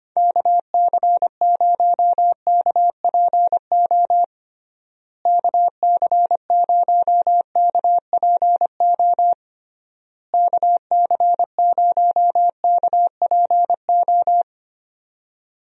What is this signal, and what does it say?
Morse code, heard as a keyed tone, about 25 wpm, 700 Hz. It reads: KC0KPO KC0KPO KC0KPO
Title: USA Call Signs - 6 Characters - Repeated 3x - 25wpm